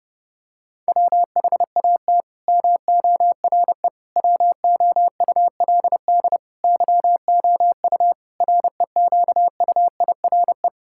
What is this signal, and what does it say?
Morse code, heard as a keyed tone, about 32 wpm, 700 Hz. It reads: WHAT MORE WOULD YOU REQUIRE
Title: Sentences from Top 1000 Words - Review Part 6 - 30wpm